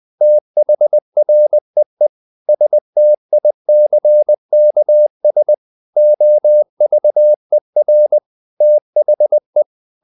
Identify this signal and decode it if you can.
Morse code, heard as a keyed tone, about 20 wpm, 600 Hz. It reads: THREE STICKS OVER THE
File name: the old man and the sea / the old man and the sea part 1 20wpm